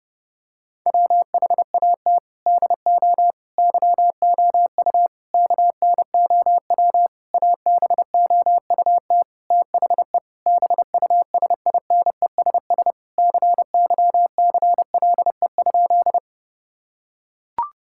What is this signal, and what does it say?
Morse code, heard as a keyed tone, about 32 wpm, 700 Hz. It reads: WHAT DO YOU KNOW ABOUT THE BUSINESS CYCLE?
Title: Sentences from Top 1000 Words - Review Part 6 - 30wpm